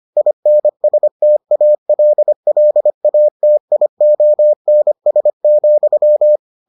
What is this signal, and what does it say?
Morse code, heard as a keyed tone, about 25 wpm, 600 Hz. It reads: INSTALLATIONS,